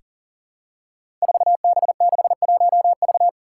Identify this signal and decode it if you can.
Morse code, heard as a keyed tone, about 40 wpm, 700 Hz. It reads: 4B61V
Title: Five Challenging Letter-Number Combinations 40wpm